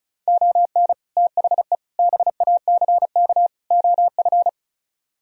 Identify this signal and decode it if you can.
Morse code, heard as a keyed tone, about 43 wpm, 700 Hz. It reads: ON THE BACK OF
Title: Speed-Racing - Sets of 4 Words 35wpm